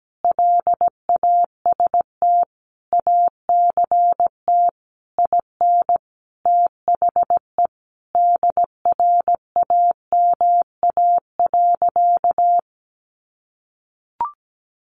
Morse code, heard as a keyed tone, about 17 wpm, 700 Hz. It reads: LAST ACT IN THE DRAMA.
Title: Sentences from Top 1500 Words 17wpm